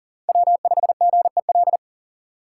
Morse code, heard as a keyed tone, about 48 wpm, 700 Hz. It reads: W5GEL
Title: Speed Racing - USA 1x3 Call Signs 40wpm